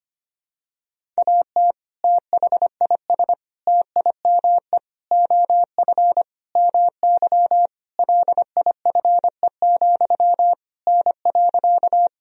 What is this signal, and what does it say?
Morse code, heard as a keyed tone, about 25 wpm, 700 Hz. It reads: AT THIS TIME OF MY LIFE, N.
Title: Sentences from Top 300 Words and Abbreviations 25wpm